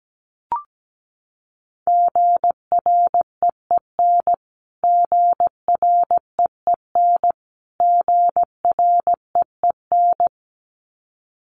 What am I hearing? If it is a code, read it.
Morse code, heard as a keyed tone, about 17 wpm, 700 Hz. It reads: GREEN GREEN GREEN
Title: Top 500 Words Repeated 17wpm